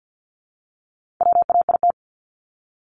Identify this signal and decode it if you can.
Morse code, heard as a keyed tone, about 50 wpm, 700 Hz. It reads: WAIT
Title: Top 500 Words Repeated 50wpm